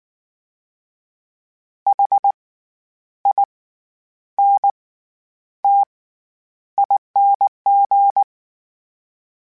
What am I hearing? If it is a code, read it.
Morse code, heard as a keyed tone, about 19 wpm, spaced as for 9 wpm, 800 Hz. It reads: HINTING
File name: Special INGs / Special INGs-mp3s / SlowToFaster ing @18